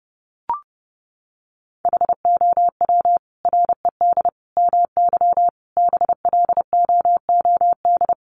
Morse code, heard as a keyed tone, about 30 wpm, 700 Hz. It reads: HOW RED MY BLOOD
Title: Sentences from Top 1100 Words 30wpm